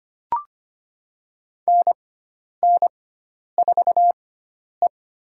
Morse code, heard as a keyed tone, about 25 wpm, spaced as for 10 wpm, 700 Hz. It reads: NN4E
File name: Lesson 16g - Letter c - Call Signs 25wpm